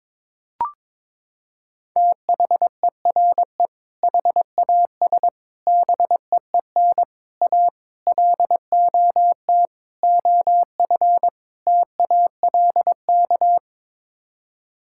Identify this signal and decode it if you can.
Morse code, heard as a keyed tone, about 22 wpm, 700 Hz. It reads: THERE HAS BEEN A LOT OF TALK
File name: practice / 22wpm / Sets of 7 Words - 22wpm